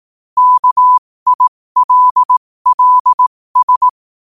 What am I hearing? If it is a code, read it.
Morse code, heard as a keyed tone, about 18 wpm, spaced as for 15 wpm, 1000 Hz. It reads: KILLS
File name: qotdfiles / 15wpm